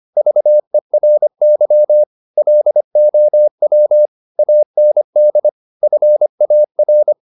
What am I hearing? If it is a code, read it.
Morse code, heard as a keyed tone, about 25 wpm, 600 Hz. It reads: VERY LOW AND FAR
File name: the old man and the sea part 1 25wpm